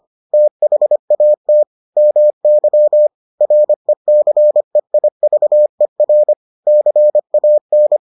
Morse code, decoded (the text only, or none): THAT MY RECEIVER CAN